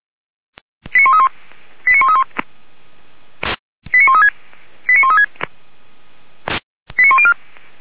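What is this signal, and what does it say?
Five-tone selective call (ZVEI), heard as quick runs of five tones, tones of about 70 ms. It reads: 89131 89131 89136 89136 89104
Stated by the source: ZVEI five-tone call received by SDR, feuerwehr-orlamuende.org fire brigade.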